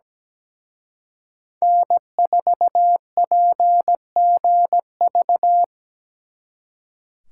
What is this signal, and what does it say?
Morse code, heard as a keyed tone, about 20 wpm, 700 Hz. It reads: N4PGV
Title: Speed Racing - USA 1x3 Call Signs 17wpm